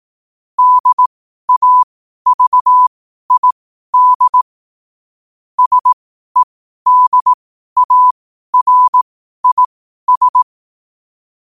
Morse code, heard as a keyed tone, about 18 wpm, spaced as for 12 wpm, 1000 Hz. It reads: DAVID SEDARIS
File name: qotdfiles / 12wpm